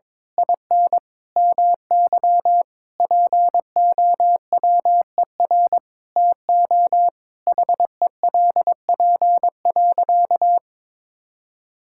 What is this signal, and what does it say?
Morse code, heard as a keyed tone, about 22 wpm, 700 Hz. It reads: IN MY POWER TO HELP.
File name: practice / 22wpm / Sentences with 8-Word Phrases - 22wpm